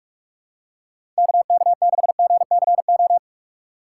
Morse code, heard as a keyed tone, about 46 wpm, 700 Hz. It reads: KK6GCO
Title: USA Call Signs - 4x Character Spacing - 45wpm